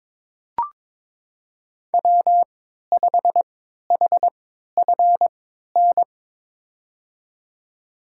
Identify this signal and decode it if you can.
Morse code, heard as a keyed tone, about 22 wpm, spaced as for 12 wpm, 700 Hz. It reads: W5HFN
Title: USA Call Signs - 5 Characters - 3x Character Spacing - 22wpm